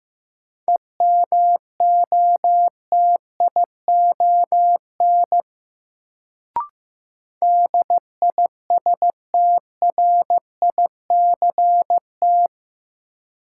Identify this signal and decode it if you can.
Morse code, heard as a keyed tone, about 15 wpm, 700 Hz. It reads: EMOTION DISTRICT
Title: Sentences from Top 1600 Words - New Word Drill - 15wpm